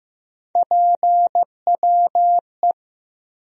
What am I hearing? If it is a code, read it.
Morse code, heard as a keyed tone, about 15 wpm, 700 Hz. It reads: PWE
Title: USA Call Sign Postfixes - 3 Letters 15wpm